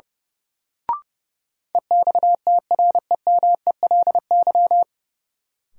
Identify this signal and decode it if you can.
Morse code, heard as a keyed tone, about 31 wpm, 700 Hz. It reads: EXTREMELY
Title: Sentences from Top 1000 Words - Review Part 7 - New Word Drill - 30wpm